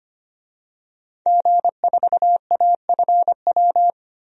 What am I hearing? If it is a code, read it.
Morse code, heard as a keyed tone, about 25 wpm, 700 Hz. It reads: G4AFW
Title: DXCC-100 Call Signs - 5 Characters - Repeated 3x - 25wpm